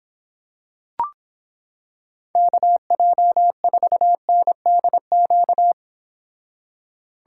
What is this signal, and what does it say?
Morse code, heard as a keyed tone, about 26 wpm, 700 Hz. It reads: KJ4NDQ